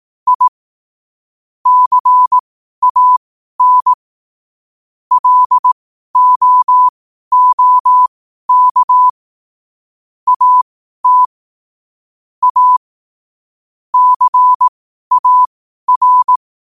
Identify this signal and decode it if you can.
Morse code, heard as a keyed tone, about 18 wpm, spaced as for 12 wpm, 1000 Hz. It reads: I CAN LOOK AT A CAR
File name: qotdfiles / 12wpm